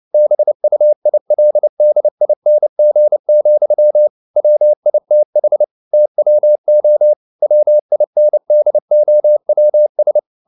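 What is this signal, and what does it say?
Morse code, heard as a keyed tone, about 29 wpm, 600 Hz. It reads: BUILDING, WITH TWO WINDOWS
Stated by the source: Morse code at 29 wpm, sending Ken Follett's novel Never in English, from CW med Gnister.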